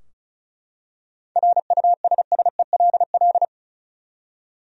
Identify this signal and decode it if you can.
Morse code, heard as a keyed tone, about 34 wpm, 700 Hz. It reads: RUSSELL
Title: USA Common Names - 35wpm